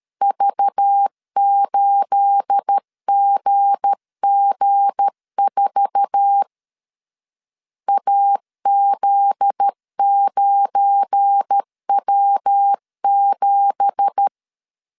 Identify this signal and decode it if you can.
Morse code, heard as a keyed tone, about 13 wpm, 800 Hz. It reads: V8GG4 AZ9W7
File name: CW / DET / 1 9 1500C S A 12wpm 3800